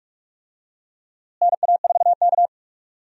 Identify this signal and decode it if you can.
Morse code, heard as a keyed tone, about 46 wpm, 700 Hz. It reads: NA4K
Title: USA Call Signs - 4 Characters - Repeated 2x - 45wpm